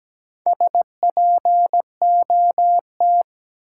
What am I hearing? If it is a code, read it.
Morse code, heard as a keyed tone, about 17 wpm, 700 Hz. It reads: SPOT
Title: Sentences from Top 1100 Words - New Word Drill Alt - 17wpm